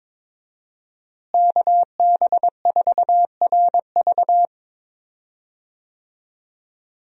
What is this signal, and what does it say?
Morse code, heard as a keyed tone, about 22 wpm, 700 Hz. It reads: KB4RV
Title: USA Call Signs - 5 Characters - Repeated 3x - 22wpm